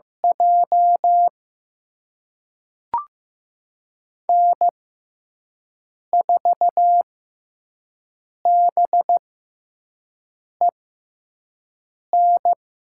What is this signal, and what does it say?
Morse code, heard as a keyed tone, about 15 wpm, spaced as for 5 wpm, 700 Hz. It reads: J N4BEN
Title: USA Call Signs - 5 Characters - 6x Character Spacing - 15wpm